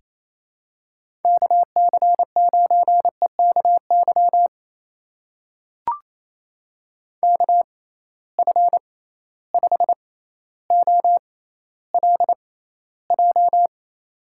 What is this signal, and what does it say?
Morse code, heard as a keyed tone, about 28 wpm, spaced as for 14 wpm, 700 Hz. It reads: KC9EKY KF5OLJ
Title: USA Call Signs - 6 Characters - 6x Character Spacing - 28wpm